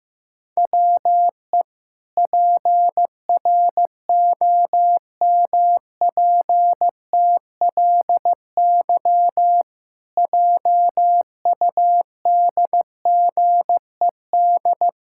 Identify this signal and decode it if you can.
Morse code, heard as a keyed tone, about 15 wpm, 700 Hz. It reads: WE PROMPTLY JUDGED
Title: Pangram Challenge v2 - 15wpm